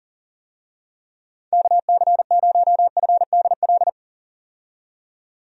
Morse code, heard as a keyed tone, about 40 wpm, 700 Hz. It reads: KC0FDL
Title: USA Call Signs - 6 Characters - 40wpm